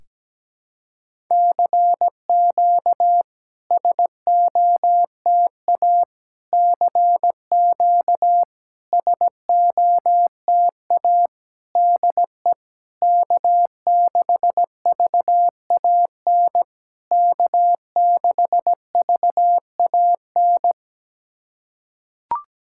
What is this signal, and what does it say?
Morse code, heard as a keyed tone, about 17 wpm, 690 Hz. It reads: CQ SOTA CQ SOTA DE K6VAN K6VAN
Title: SOTA QSO Practice 17wpm